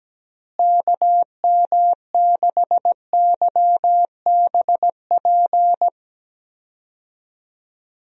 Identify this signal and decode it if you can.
Morse code, heard as a keyed tone, about 17 wpm, 700 Hz. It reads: KM6YBP
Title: USA Call Signs - 6 Characters - Repeated 3x - 17wpm